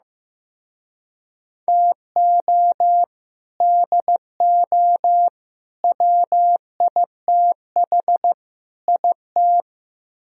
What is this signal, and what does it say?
Morse code, heard as a keyed tone, about 15 wpm, 700 Hz. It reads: TO DO WITH IT
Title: Speed-Racing - Sets of 4 Words 15wpm